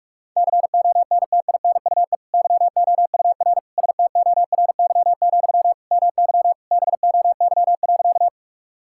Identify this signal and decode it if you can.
Morse code, heard as a keyed tone, about 46 wpm, 700 Hz. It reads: CONTINUE YOUR STORY, MY BOY.
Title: Sentences from Top 1000 Words - Review Part 6 - 45wpm